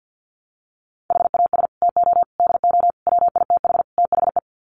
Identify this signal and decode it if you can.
Morse code, heard as a keyed tone, about 50 wpm, 700 Hz. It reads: HAS TO DO WITH THE
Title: Sets of 5 Words 50wpm